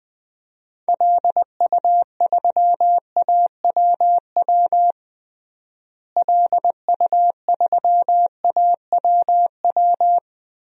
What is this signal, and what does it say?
Morse code, heard as a keyed tone, about 20 wpm, 700 Hz. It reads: LU3AWW LU3AWW
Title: DXCC-100 Call Signs - 6 Characters - Repeated 2x - 20wpm